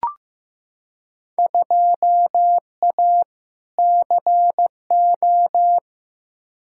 Morse code, heard as a keyed tone, about 15 wpm, 700 Hz. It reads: 2A CO
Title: Field Day Exchanges - 15wpm